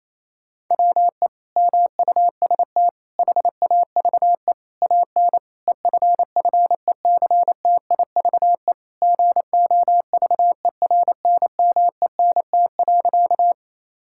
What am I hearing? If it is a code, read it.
Morse code, heard as a keyed tone, about 28 wpm, 700 Hz. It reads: WE MUST HAVE AN EFFECTIVE GOVERNMENT.
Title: Sentences from Top 1000 Words - Review Part 6 - 28wpm